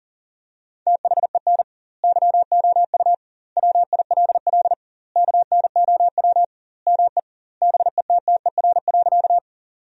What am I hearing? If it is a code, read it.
Morse code, heard as a keyed tone, about 40 wpm, 700 Hz. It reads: THEN YOU WILL KNOW ME BETTER.
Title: 2x Word-Spacing - Sentences from Top 400 Words - 40wpm